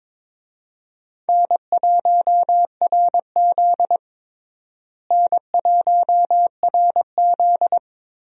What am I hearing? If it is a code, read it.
Morse code, heard as a keyed tone, about 22 wpm, 700 Hz. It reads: N1RZ N1RZ
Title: USA Call Signs - 4 Characters - Repeated 3x - 22wpm